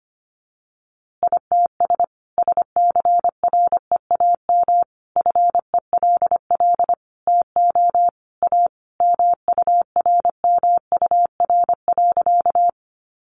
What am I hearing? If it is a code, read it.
Morse code, heard as a keyed tone, about 25 wpm, 700 Hz. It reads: ITS SCREAM FELL TO A MURMUR.
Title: A Sound Of Thunder 25wpm